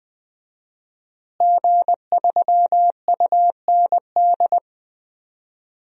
Morse code, heard as a keyed tone, about 20 wpm, 700 Hz. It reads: G3UND
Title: DXCC-100 Call Signs - 5 Characters - Repeated 3x - 20wpm